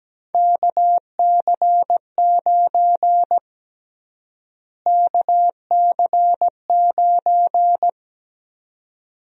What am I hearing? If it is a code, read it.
Morse code, heard as a keyed tone, about 17 wpm, 700 Hz. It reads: KC9 KC9
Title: USA Call Sign Prefixes - Repeated 2x - 17wpm